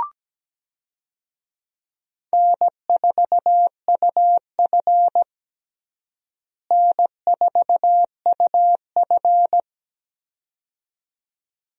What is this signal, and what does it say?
Morse code, heard as a keyed tone, about 17 wpm, 700 Hz. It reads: N4UF N4UF
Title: USA Call Signs - 4 Characters - Repeated 2x - 17wpm